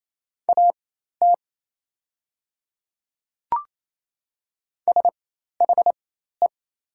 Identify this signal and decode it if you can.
Morse code, heard as a keyed tone, about 28 wpm, spaced as for 13 wpm, 700 Hz. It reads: AT SHE
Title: 4x Character Spacing - Top 100 Words - 28wpm